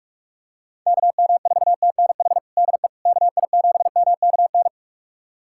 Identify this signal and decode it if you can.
Morse code, heard as a keyed tone, about 41 wpm, 700 Hz. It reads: KM4TNH DE KI7MKN
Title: USA Callsign DE Callsign - 6 Characters - 45wpm